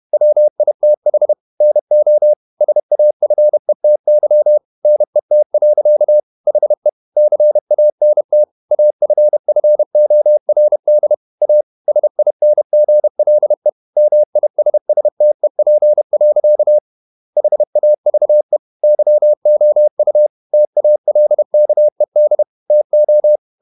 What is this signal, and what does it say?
Morse code, heard as a keyed tone, about 31 wpm, 600 Hz. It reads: WITH NO SAFETY NET. HE CANT AFFORD A SINGLE MISSTEP. HAVE YOU TALKED TO
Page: Never 31wpm | CW med Gnister